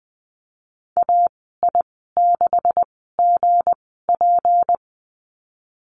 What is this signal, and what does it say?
Morse code, heard as a keyed tone, about 20 wpm, spaced as for 15 wpm, 700 Hz. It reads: AI6GP
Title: 161 Lesson 26g - Letter g - Call Signs 20wpm